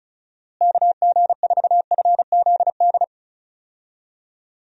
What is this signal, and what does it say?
Morse code, heard as a keyed tone, about 36 wpm, 700 Hz. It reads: KG4FZD
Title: USA Call Signs - 6 Characters - Repeated 2x - 35wpm